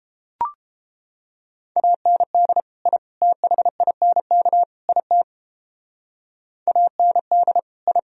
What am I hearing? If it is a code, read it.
Morse code, heard as a keyed tone, about 31 wpm, 700 Hz. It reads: AND I THINK IT AND I